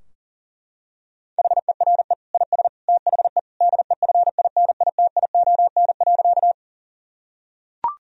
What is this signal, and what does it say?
Morse code, heard as a keyed tone, about 40 wpm, 700 Hz. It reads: HERE IS THE DEFINITION.
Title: Sentences from Top 1000 Words - Review Part 4 - 40wpm